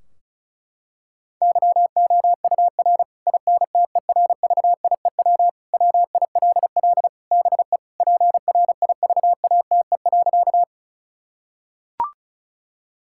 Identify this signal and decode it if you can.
Morse code, heard as a keyed tone, about 34 wpm, 700 Hz. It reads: YOUR INTERVIEW WILL BE PRIVATE.
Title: Sentences from Top 1600 Words 35wpm